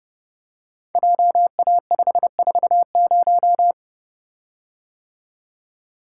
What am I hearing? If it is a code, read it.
Morse code, heard as a keyed tone, about 28 wpm, 700 Hz. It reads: JA540
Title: Code Groups - 5 Letter:Numbers - 30wpm